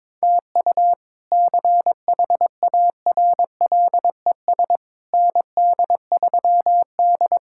Morse code, heard as a keyed tone, about 22 wpm, 700 Hz. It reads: TU CHARLES ND3D
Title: K1USN SST QSO Practice - 22wpm